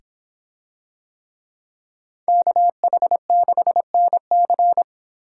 Morse code, heard as a keyed tone, about 25 wpm, 700 Hz. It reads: KH6NC